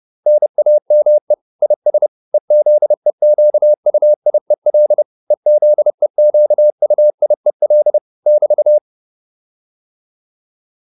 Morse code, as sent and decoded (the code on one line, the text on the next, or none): -. .- -- . / .. ... / . --.. --.- ..- .. .-.. / . --.. --.- ..- .. .-.. / -...-
NAME IS EZEQUIEL EZEQUIEL <BT>